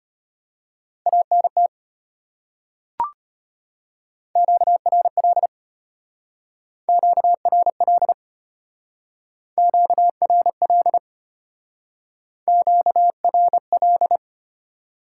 Morse code, heard as a keyed tone, about 32 wpm, 700 Hz. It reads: ANT QRL QRL QRL QRL